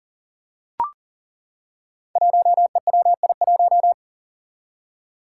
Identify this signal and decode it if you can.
Morse code, heard as a keyed tone, about 40 wpm, 700 Hz. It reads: 1EWI1
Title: Five Challenging Letter-Number Combinations 40wpm